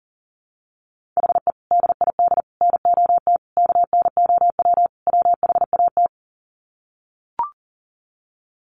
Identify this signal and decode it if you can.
Morse code, heard as a keyed tone, about 40 wpm, 700 Hz. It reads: HE DID NOT KNOW WHAT
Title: Sets of 5 Words Encore 40wpm